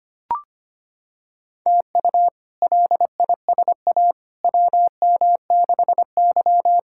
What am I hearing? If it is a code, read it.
Morse code, heard as a keyed tone, about 25 wpm, 700 Hz. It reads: TU LISA WM6Y
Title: K1USN SST QSO Practice - 25wpm